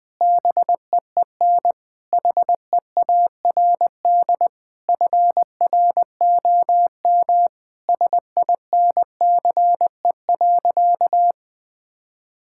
Morse code, heard as a keyed tone, about 21 wpm, 700 Hz. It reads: BEEN HEARD FROM SINCE.
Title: Sentences from Top 1600 Words 20wpm